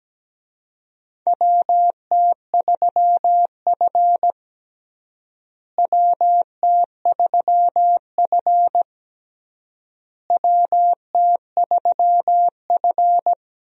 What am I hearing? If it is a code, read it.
Morse code, heard as a keyed tone, about 17 wpm, 700 Hz. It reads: WT3F WT3F WT3F
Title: USA Call Signs - 4 Characters - Repeated 3x - 17wpm